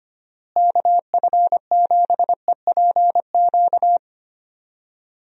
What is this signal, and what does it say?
Morse code, heard as a keyed tone, about 25 wpm, 700 Hz. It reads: KF7EPQ
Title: USA Call Signs - 6 Characters - Repeated 2x - 25wpm